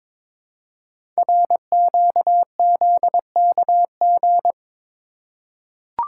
Morse code, heard as a keyed tone, about 22 wpm, 700 Hz. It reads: RQZKG
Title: Code Groups - 5 Letter:Numbers - 22wpm